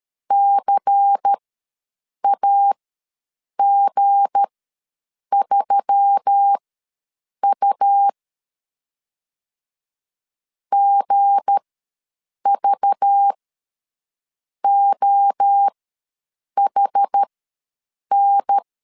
Morse code, heard as a keyed tone, about 13 wpm, spaced as for 7 wpm, 800 Hz. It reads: CAG3U GVOHN
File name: LEZIONI / 7 GF9Z / 7 G ws 12wpm 3140